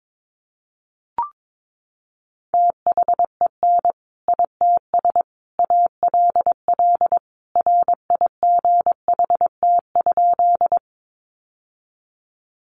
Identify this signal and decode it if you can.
Morse code, heard as a keyed tone, about 22 wpm, 700 Hz. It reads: THEN ITS ALL RIGHT?
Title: Sentences from Top 600 Words 22wpm